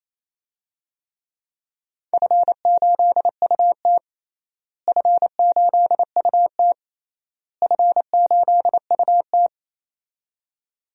Morse code, heard as a keyed tone, about 28 wpm, 700 Hz. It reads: F8UT F8UT F8UT
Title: DXCC-100 Call Signs - 4 Characters - Repeated 3x - 28wpm